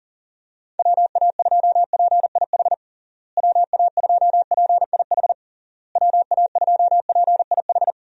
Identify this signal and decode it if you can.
Morse code, heard as a keyed tone, about 37 wpm, 700 Hz. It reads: WA2PIH WA2PIH WA2PIH
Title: USA Call Signs - 6 Characters - Repeated 3x - 40wpm